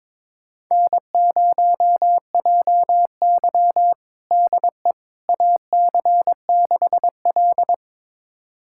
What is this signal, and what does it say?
Morse code, heard as a keyed tone, about 22 wpm, 700 Hz. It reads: N0JY DE AC6L
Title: USA Callsign DE Callsign - 4 Characters - 22wpm